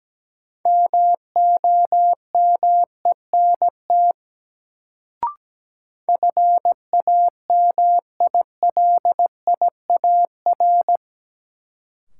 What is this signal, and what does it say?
Morse code, heard as a keyed tone, about 17 wpm, 700 Hz. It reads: MOMENT FAMILIAR
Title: Sentences from Top 1000 Words - New Word Drill - 17wpm